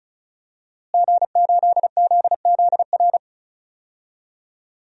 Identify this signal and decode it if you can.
Morse code, heard as a keyed tone, about 36 wpm, 700 Hz. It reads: G8ZZR
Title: DXCC-100 Call Signs - 5 Characters - Repeated 3x - 35wpm